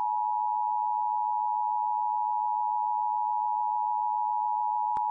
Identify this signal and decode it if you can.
Emergency Alert System (SAME) tones (the 853 and 960 Hz attention tone). It is heard from 0.4 to 4.5 s.